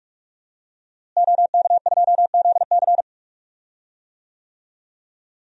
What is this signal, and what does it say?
Morse code, heard as a keyed tone, about 46 wpm, 700 Hz. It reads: OK2ZC
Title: DXCC-100 Call Signs - 5 Characters - Repeated 2x - 45wpm